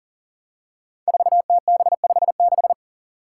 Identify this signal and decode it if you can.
Morse code, heard as a keyed tone, about 40 wpm, 700 Hz. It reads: 4TB56
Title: Five Challenging Letter-Number Combinations 40wpm